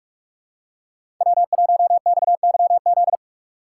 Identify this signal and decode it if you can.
Morse code, heard as a keyed tone, about 44 wpm, 700 Hz. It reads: W1XYZ
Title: USA Call Signs - 5 Characters - 3x Character Spacing - 45wpm